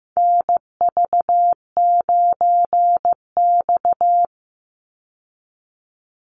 Morse code, heard as a keyed tone, about 15 wpm, 700 Hz. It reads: NV9X
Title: Call Signs Repeated 2x - 15wpm